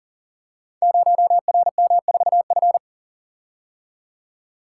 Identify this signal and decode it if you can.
Morse code, heard as a keyed tone, about 40 wpm, 700 Hz. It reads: 0RM4F
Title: Code Groups - 5 Letter:Numbers - 40wpm